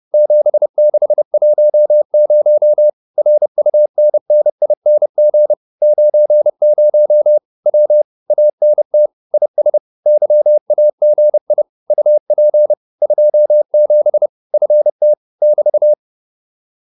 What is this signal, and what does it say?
Morse code, heard as a keyed tone, about 30 wpm, 600 Hz. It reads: 7610 RUNNING 90 W ANT IS YAGI UP 27 FT <BT>